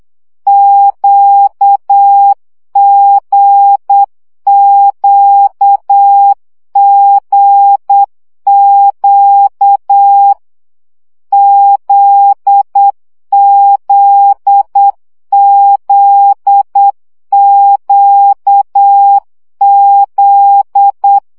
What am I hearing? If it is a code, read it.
Morse code, heard as a keyed tone, about 8 wpm, 800 Hz. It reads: QGQGQ ZZZQZ
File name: CW gyakorlas g z q 7 baud